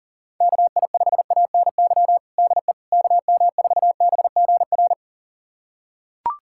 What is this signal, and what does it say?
Morse code, heard as a keyed tone, about 40 wpm, 700 Hz. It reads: KI5ANY DE KM4BGR
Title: USA Callsign DE Callsign 40wpm